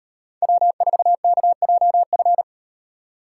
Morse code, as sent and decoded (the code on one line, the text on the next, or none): .-- ....- -.- .--- ..-.
W4KJF